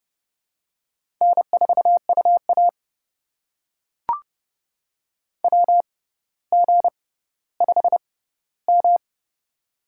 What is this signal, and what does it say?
Morse code, heard as a keyed tone, about 30 wpm, spaced as for 15 wpm, 700 Hz. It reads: N4UA WG5M
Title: USA Call Signs - 4 Characters - 6x Character Spacing - 30wpm